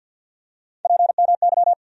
Morse code, heard as a keyed tone, about 54 wpm, 700 Hz. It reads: PMY
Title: USA Call Sign Postfixes - 3 Letters 50wpm